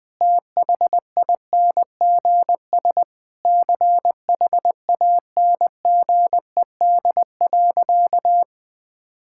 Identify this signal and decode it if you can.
Morse code, heard as a keyed tone, about 21 wpm, 700 Hz. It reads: THINGS CHANGED.
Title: Sentences with 4-Word Phrases - 20wpm